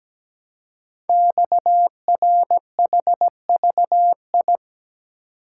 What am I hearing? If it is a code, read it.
Morse code, heard as a keyed tone, about 17 wpm, 700 Hz. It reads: XRHVI
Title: Code Groups - 5 Letters - 17wpm